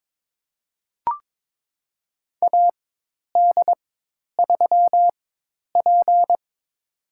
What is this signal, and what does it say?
Morse code, heard as a keyed tone, about 23 wpm, spaced as for 10 wpm, 700 Hz. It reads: AD3P
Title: USA Call Signs - 4 Characters - 4x Character Spacing - 22wpm